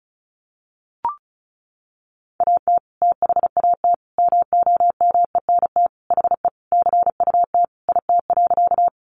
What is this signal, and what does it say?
Morse code, heard as a keyed tone, about 34 wpm, 700 Hz. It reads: AT THAT MOMENT HE CUT IT.
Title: Sentences from Top 1000 Words 35wpm